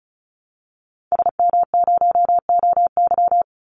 Morse code, heard as a keyed tone, about 34 wpm, 700 Hz. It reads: SM0OY
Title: Call Signs Repeated 3x - 35wpm